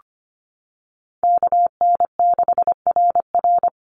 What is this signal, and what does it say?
Morse code, heard as a keyed tone, about 25 wpm, 700 Hz. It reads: KN6RR
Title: USA Call Signs 25wpm